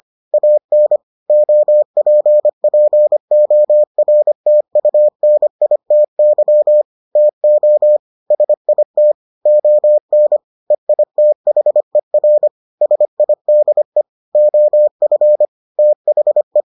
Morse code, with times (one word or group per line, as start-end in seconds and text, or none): 0.3-1.0 AN
1.3-6.8 OPPORTUNITY
7.2-8.0 TO
8.3-9.1 SIT
9.5-10.4 ON
10.7-12.5 EITHER
12.8-14.0 SIDE
14.3-15.4 OF
15.8-16.6 THE